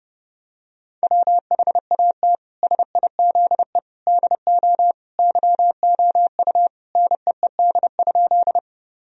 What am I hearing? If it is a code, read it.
Morse code, heard as a keyed tone, about 30 wpm, 700 Hz. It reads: WHAT SIZE DO YOU NEED?
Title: Sentences from Top 1000 Words - Review Part 2 - 30wpm